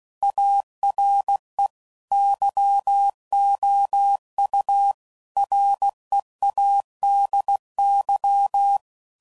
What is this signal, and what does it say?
Morse code, heard as a keyed tone, about 16 wpm, 800 Hz. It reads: ARE YOU READY
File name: CW Code Groups @15WPM / Technical Subjects / OR Gate